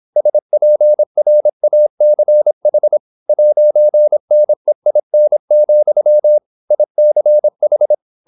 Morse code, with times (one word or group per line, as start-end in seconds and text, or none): 0.2-3.0 SPRACH
3.3-6.4 'NEIN,
6.7-7.9 ICH